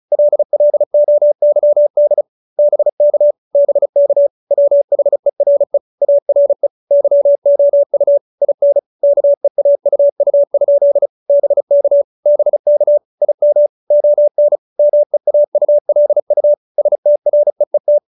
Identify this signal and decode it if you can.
Morse code, heard as a keyed tone, about 34 wpm, 590 Hz. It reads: LLOYD BK BK WHERE ARE YOU IN KEAUU? BK BK IM ON MEAULU STREET